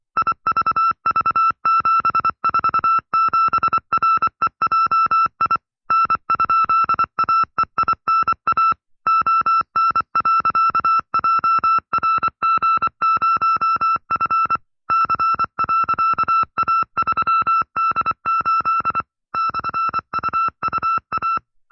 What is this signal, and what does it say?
Morse code, heard as a keyed tone, about 24 wpm, 1330 Hz. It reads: IVV747REJI N?AEINA ON.JRG0F C.A3D8 /UUA
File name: FreeFiles / CW-practice / 20WPM-side2